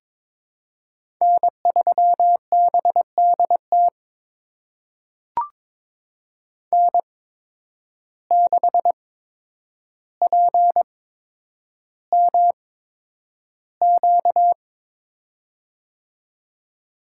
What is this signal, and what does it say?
Morse code, heard as a keyed tone, about 22 wpm, spaced as for 10 wpm, 700 Hz. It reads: N3BDT N6PMQ
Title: USA Call Signs - 5 Characters - 8x Character Spacing - 22wpm